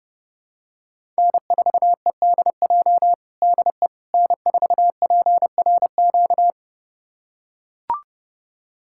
Morse code, heard as a keyed tone, about 30 wpm, 700 Hz. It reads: N4EDJ DE N4PRQ
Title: USA Callsign DE Callsign - 5 Characters - 30wpm